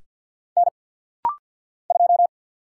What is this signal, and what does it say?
Morse code, heard as a keyed tone, about 49 wpm, 700 Hz. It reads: N 2
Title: Single Letter-Number Alt - Rapid-Fire 50wpm